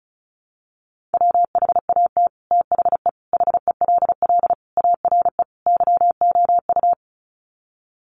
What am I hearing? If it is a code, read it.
Morse code, heard as a keyed tone, about 34 wpm, 700 Hz. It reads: WHAT THE HELL ARE YOU
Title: Sets of 5 Words Encore 35wpm